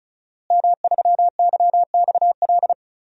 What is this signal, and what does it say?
Morse code, heard as a keyed tone, about 36 wpm, 700 Hz. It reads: M3YXL
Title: DXCC-100 Call Signs - 5 Characters - Repeated 2x - 35wpm